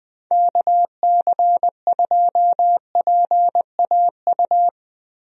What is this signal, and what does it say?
Morse code, heard as a keyed tone, about 20 wpm, 700 Hz. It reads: KC2PAU
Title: USA Call Signs - 6 Characters - 20wpm